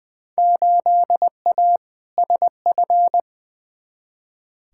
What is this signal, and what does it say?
Morse code, heard as a keyed tone, about 20 wpm, 700 Hz. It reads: 8A SF
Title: Field Day Exchanges Encore - 20wpm